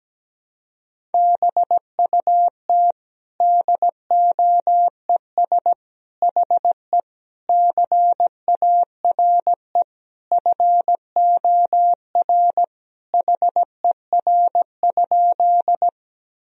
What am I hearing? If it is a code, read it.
Morse code, heard as a keyed tone, about 17 wpm, 700 Hz. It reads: BUT DOES HE CARE FOR HER?
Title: Sentences from Top 1000 Words - Review Part 1 - 17wpm